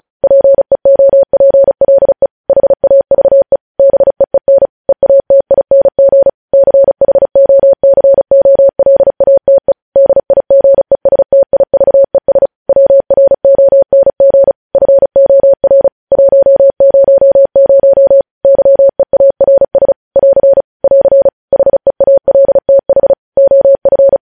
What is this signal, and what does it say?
Morse code, heard as a keyed tone, about 36 wpm, 550 Hz. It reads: PEOPLE HAVE BEEN EATING CHOCOLATE DIGESTIVES WRONG FOR 100 YEARS <AR> <AR> HEALTH OF
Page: News Headlines in Morse Code at 35 WPM